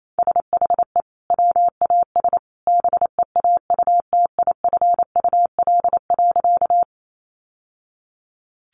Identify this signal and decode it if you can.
Morse code, heard as a keyed tone, about 28 wpm, 700 Hz. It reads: SHE WAS BEAUTIFUL.
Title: Sentences from Top 900 Words 28wpm